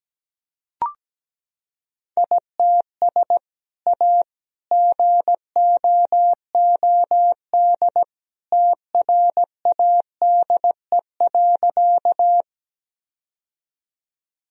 Morse code, heard as a keyed tone, about 17 wpm, 700 Hz. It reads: ITS A GOOD TRADE.
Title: Sentences from Top 1000 Words - Review Part 4 - 17wpm